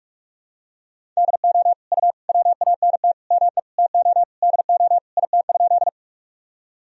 Morse code, heard as a keyed tone, about 42 wpm, 700 Hz. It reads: DO U WANT ME TO DO IT?
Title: Sentences from Top 200 Words and Abbreviations 45wpm